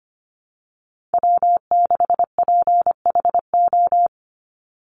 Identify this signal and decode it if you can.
Morse code, heard as a keyed tone, about 25 wpm, 700 Hz. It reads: W6PHO
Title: Call Signs Repeated 2x - 25wpm